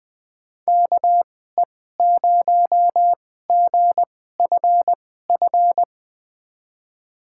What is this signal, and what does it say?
Morse code, heard as a keyed tone, about 21 wpm, spaced as for 15 wpm, 700 Hz. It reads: KE0GFF
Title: USA Call Signs - 6 Characters - 2x Character Spacing - 20wpm